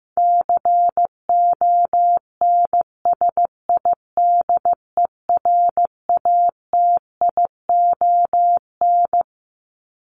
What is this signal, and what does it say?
Morse code, heard as a keyed tone, about 15 wpm, 700 Hz. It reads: CONSIDERATION
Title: Top 1251-1500 Words 15wpm